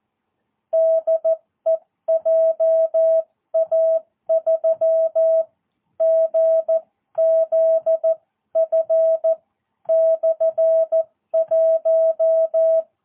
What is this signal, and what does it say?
Morse code, heard as a keyed tone, about 14 wpm, spaced as for 11 wpm, 660 Hz. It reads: DE JA3GZF/1
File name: jh8cbh / 2022cw